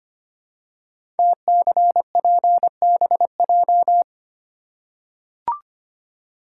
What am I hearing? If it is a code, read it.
Morse code, heard as a keyed tone, about 25 wpm, 700 Hz. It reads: TCPBJ
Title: Code Groups - 5 Letters - 25wpm